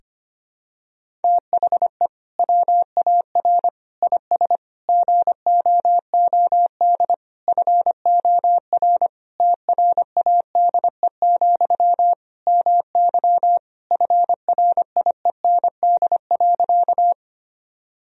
Morse code, heard as a keyed tone, about 25 wpm, 700 Hz. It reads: THE WAR IS GOOD FOR TRADE, MY FRIEND.
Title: Sentences from Top 1000 Words - Review Part 4 - 25wpm